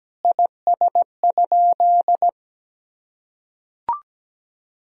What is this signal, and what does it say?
Morse code, heard as a keyed tone, about 17 wpm, 700 Hz. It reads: IS?
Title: Sentences from Top 100 Words and Abbreviations 17wpm